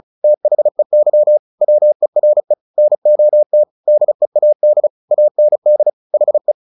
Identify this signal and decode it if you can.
Morse code, heard as a keyed tone, about 34 wpm, 600 Hz. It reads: THEY WERE NOT DEAD AND HE
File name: the old man and the sea 2 35wpm